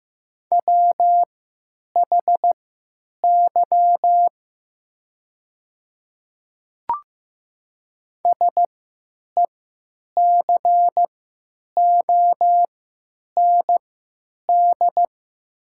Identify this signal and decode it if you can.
Morse code, heard as a keyed tone, about 15 wpm, spaced as for 9 wpm, 700 Hz. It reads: WHY SECOND
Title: 3x Character Spacing - Top 500 Words - 15wpm